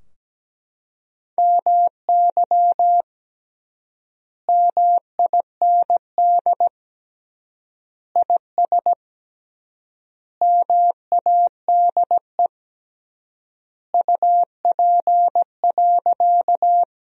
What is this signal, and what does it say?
Morse code, heard as a keyed tone, about 17 wpm, 700 Hz. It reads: MY MIND IS MADE UP.
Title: 3x Word-Spacing - Sentences from Top 500 Words - 17wpm